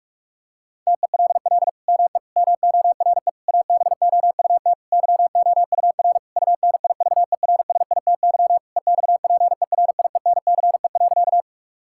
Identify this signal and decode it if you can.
Morse code, heard as a keyed tone, about 41 wpm, 700 Hz. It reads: TELL ME MORE ABOUT YOUR UNIVERSITY EXPERIENCE.
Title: Sentences from Top 1000 Words - Review Part 5 - 45wpm